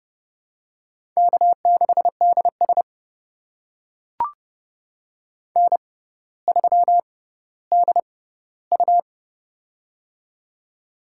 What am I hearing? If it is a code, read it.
Morse code, heard as a keyed tone, about 30 wpm, spaced as for 15 wpm, 700 Hz. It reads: K6DS N3DU
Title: USA Call Signs - 6x Character Spacing - 30wpm